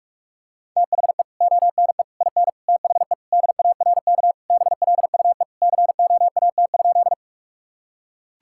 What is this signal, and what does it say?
Morse code, heard as a keyed tone, about 46 wpm, 700 Hz. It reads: THE ONE IN THE DARK BLUE COAT?
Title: Sentences from Top 1600 Words 45wpm